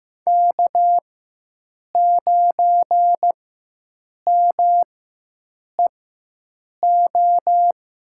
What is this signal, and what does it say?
Morse code, heard as a keyed tone, about 15 wpm, spaced as for 7 wpm, 700 Hz. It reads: K9MEO
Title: USA Call Signs - 5 Characters - 4x Character Spacing - 15wpm